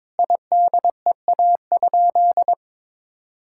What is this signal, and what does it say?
Morse code, heard as a keyed tone, about 22 wpm, 700 Hz. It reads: IDEA?
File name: Sentences with 7-Word Phrases - 22wpm